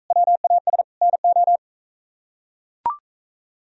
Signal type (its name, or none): Morse code